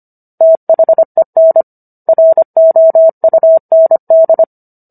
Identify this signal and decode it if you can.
Morse code, heard as a keyed tone, about 25 wpm, 650 Hz. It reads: THEN ROUND